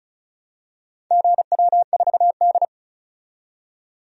Morse code, heard as a keyed tone, about 34 wpm, 700 Hz. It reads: GW4D
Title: DXCC-100 Call Signs - 4 Characters - 35wpm